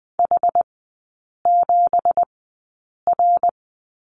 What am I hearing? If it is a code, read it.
Morse code, heard as a keyed tone, about 20 wpm, spaced as for 8 wpm, 700 Hz. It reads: H7R
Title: 166 Lesson 27e - Number 7 - Three Characters 20wpm